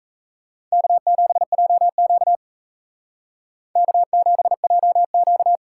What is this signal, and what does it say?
Morse code, heard as a keyed tone, about 36 wpm, 700 Hz. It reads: K7JQ K7JQ